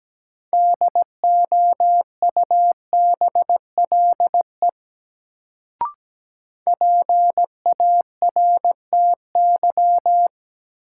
Morse code, heard as a keyed tone, about 17 wpm, 700 Hz. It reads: DOUBLE PARTY
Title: Sentences from Top 1400 Words - New Word Drill - 17wpm